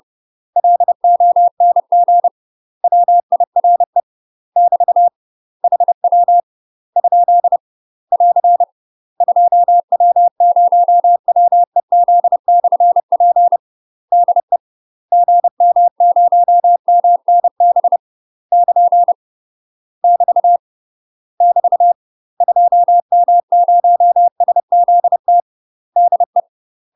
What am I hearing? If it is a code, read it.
Morse code, heard as a keyed tone, about 30 wpm, 700 Hz. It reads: LONG WIRE <BT> HW ? <AR> 2W0WEZ/P DE GM0MNB <KN> <BT> <BT> 2M0SZT DE
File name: UK MASTER QSOs TRAINING / 30WPM / UK MASTER QSOs 0045